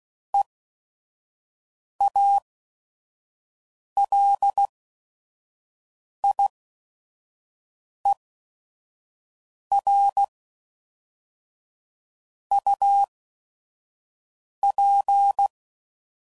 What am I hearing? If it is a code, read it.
Morse code, heard as a keyed tone, about 16 wpm, spaced as for 5 wpm, 800 Hz. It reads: EALIER UP
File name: CW Code Groups @5WPM / Weather Sets / Weather 5 5